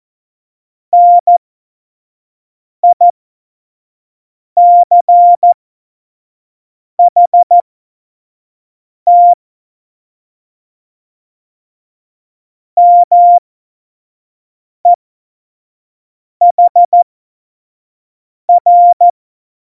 Morse code, heard as a keyed tone, about 14 wpm, spaced as for 5 wpm, 700 Hz. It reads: NICHT MEHR
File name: Morsekurs / Witze / Witz 19 Mai 2020, 25 F70